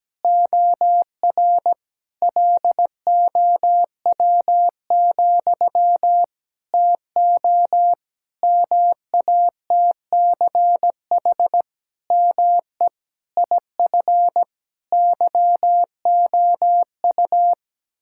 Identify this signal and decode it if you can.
Morse code, heard as a keyed tone, about 17 wpm, 700 Hz. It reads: OR LOW, TO MATCH ME IF YOU
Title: Sentences from Top 1600 Words 17wpm